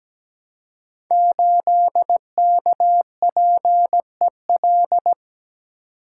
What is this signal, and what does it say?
Morse code, heard as a keyed tone, about 17 wpm, 700 Hz. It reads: 8KPEL
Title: Code Groups - 5 Letter:Numbers - 17wpm